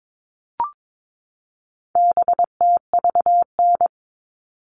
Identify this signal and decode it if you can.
Morse code, heard as a keyed tone, about 22 wpm, 700 Hz. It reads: BTVN
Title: All Cut Numbers - 4-Digit 22wpm